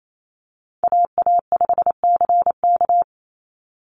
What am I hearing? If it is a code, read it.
Morse code, heard as a keyed tone, about 28 wpm, 700 Hz. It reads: AA5CK
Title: NA SOTA Chaser Callsigns 28wpm